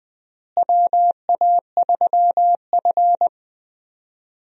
Morse code, heard as a keyed tone, about 20 wpm, 700 Hz. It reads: WA3F
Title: USA Call Signs - 4 Characters - Repeated 3x - 20wpm